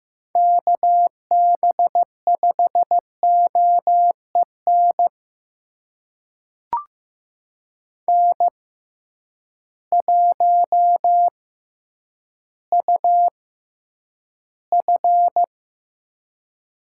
Morse code, heard as a keyed tone, about 15 wpm, spaced as for 9 wpm, 700 Hz. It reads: KB5OEN N1UF
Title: USA Call Signs - 6x Character Spacing - 15wpm